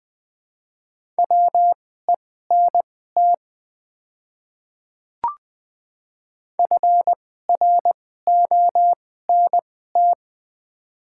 Morse code, heard as a keyed tone, about 20 wpm, spaced as for 15 wpm, 700 Hz. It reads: WENT FRONT
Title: 2x Character Spacing - Top 500 Words - 20wpm